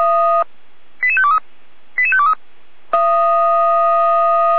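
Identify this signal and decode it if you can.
Five-tone selective call (ZVEI), heard as quick runs of five tones, tones of about 70 ms. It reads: 88523 88523
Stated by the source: Five-tone ZVEI selective call received on an SDR.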